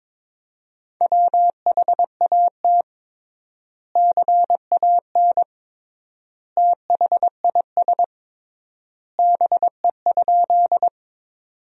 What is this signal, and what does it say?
Morse code, heard as a keyed tone, about 22 wpm, 700 Hz. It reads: WHAT CAN THIS BE?
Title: 3x Word-Spacing - Sentences from Top 100 Words - 22wpm